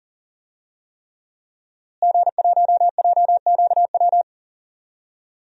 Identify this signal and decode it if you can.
Morse code, heard as a keyed tone, about 40 wpm, 700 Hz. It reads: G1JQW
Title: DXCC-100 Call Signs - 5 Characters - Repeated 3x - 40wpm